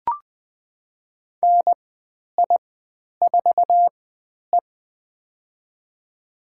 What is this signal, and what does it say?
Morse code, heard as a keyed tone, about 20 wpm, spaced as for 10 wpm, 700 Hz. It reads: NI4E
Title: 102 Lesson 17g - Letter u - Call Signs 20wpm